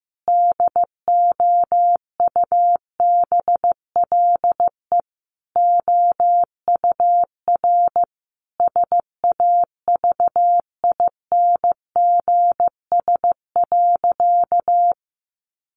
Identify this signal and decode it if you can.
Morse code, heard as a keyed tone, about 15 wpm, 700 Hz. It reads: DOUBLE OUR SAVINGS.
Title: Sentences from Top 1400 Words 15wpm